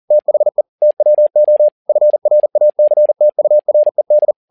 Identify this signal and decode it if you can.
Morse code, heard as a keyed tone, about 40 wpm, 600 Hz. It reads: THE TWO FRACTURED
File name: learning center / tower trouble / tower touble 40wpm